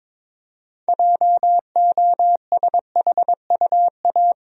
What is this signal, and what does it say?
Morse code, heard as a keyed tone, about 22 wpm, 700 Hz. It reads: JOSHUA
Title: USA Common Names - 22wpm